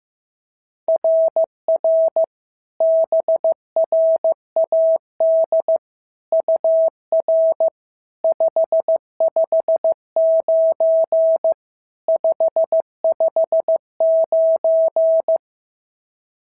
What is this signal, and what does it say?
Morse code, heard as a keyed tone, about 14 wpm, 650 Hz. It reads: RR BRAD UR 559 559
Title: SOTA QSO Practice 15wpm